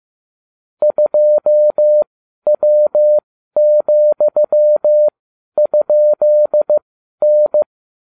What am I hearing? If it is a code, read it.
Morse code, heard as a keyed tone, about 15 wpm, spaced as for 11 wpm, 600 Hz. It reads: 2W,?N